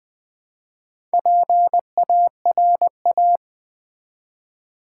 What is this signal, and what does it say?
Morse code, heard as a keyed tone, about 20 wpm, 700 Hz. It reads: PARA
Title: Build A Word - 2 Letters At A Time - 6-Letter Words - 20wpm